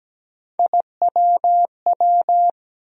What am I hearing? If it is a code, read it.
Morse code, heard as a keyed tone, about 17 wpm, 700 Hz. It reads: IWW
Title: USA Call Sign Postfixes - 3 Letters 17wpm